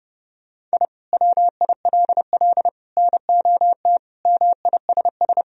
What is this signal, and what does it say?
Morse code, heard as a keyed tone, about 30 wpm, 700 Hz. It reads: I WILL NOT MISS
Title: Sentences from Top 1000 Words - Review Part 3 - 30wpm